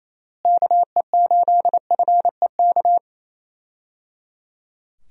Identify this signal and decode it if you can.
Morse code, heard as a keyed tone, about 28 wpm, 700 Hz. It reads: KE8FEK
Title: USA Call Signs - 6 Characters - Repeated 3x - 28wpm